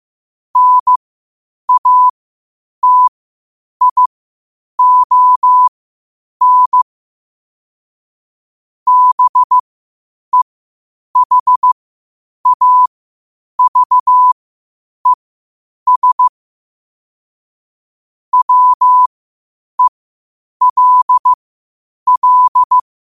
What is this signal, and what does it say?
Morse code, heard as a keyed tone, about 15 wpm, spaced as for 8 wpm, 1000 Hz. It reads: NATION BEHAVES WELL
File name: qotdfiles / 8wpm